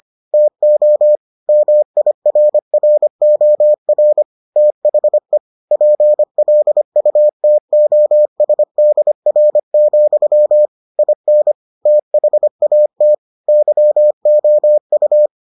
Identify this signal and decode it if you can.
Morse code, heard as a keyed tone, about 25 wpm, 600 Hz. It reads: TO MIRROR THE PLUTOSDR, IN THAT YOU